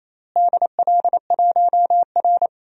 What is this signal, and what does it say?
Morse code, heard as a keyed tone, about 28 wpm, 700 Hz. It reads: DL1R
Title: DXCC-100 Call Signs - 4 Characters - Repeated 3x - 28wpm